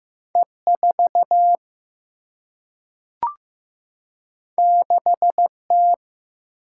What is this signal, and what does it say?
Morse code, heard as a keyed tone, about 15 wpm, 700 Hz. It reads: E 4 6 T
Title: Single-Letter Single-Number 15wpm